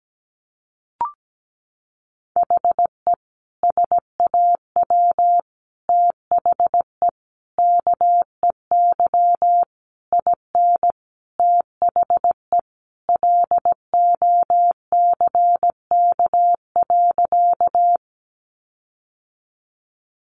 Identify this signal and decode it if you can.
Morse code, heard as a keyed tone, about 17 wpm, 700 Hz. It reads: HE SAW THE KEY IN THE LOCK.
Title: Sentences from Top 1300 Words 17wpm